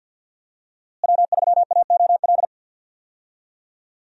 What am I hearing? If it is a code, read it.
Morse code, heard as a keyed tone, about 49 wpm, 700 Hz. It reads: W3AOL
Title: USA Call Signs - 5 Characters - Repeated 2x - 50wpm